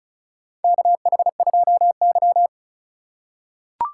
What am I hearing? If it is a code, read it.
Morse code, heard as a keyed tone, about 36 wpm, 700 Hz. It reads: KH2Y
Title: USA Call Signs - 4 Characters - 2x Character Spacing - 35wpm